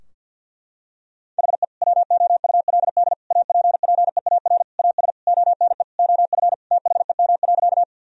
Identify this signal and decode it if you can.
Morse code, heard as a keyed tone, about 52 wpm, 700 Hz. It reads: HE WOULD APPEAR AS ONE OF THEM.
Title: Sentences from Top 1000 Words - Review Part 2 - 50wpm